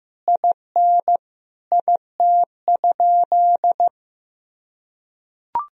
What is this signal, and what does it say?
Morse code, heard as a keyed tone, about 16 wpm, 700 Hz. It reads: IN IT?
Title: Sentences from Top 1000 Words - Review Part 4 - 15wpm